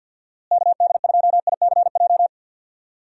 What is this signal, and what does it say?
Morse code, heard as a keyed tone, about 52 wpm, 700 Hz. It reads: KD2ICJ
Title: USA Call Signs - 6 Characters - Repeated 3x - 50wpm